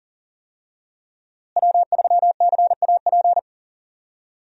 Morse code, heard as a keyed tone, about 37 wpm, 700 Hz. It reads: W3CAP
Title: USA Call Signs - 5 Characters - Repeated 3x - 40wpm